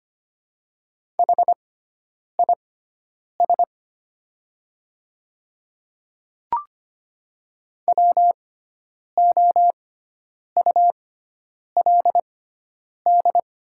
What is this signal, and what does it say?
Morse code, heard as a keyed tone, about 25 wpm, spaced as for 9 wpm, 700 Hz. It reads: HIS WOULD
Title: 6x Character Spacing - Top 200 Words - 25wpm